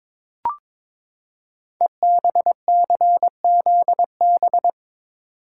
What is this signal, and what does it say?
Morse code, heard as a keyed tone, about 22 wpm, 700 Hz. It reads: EBCZB
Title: Code Groups - 5 Letters - 22wpm